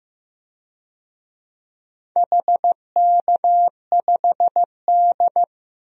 Morse code, heard as a keyed tone, about 15 wpm, 700 Hz. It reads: HK5D
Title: DXCC-100 Call Signs - 4 Characters - Repeated 2x - 15wpm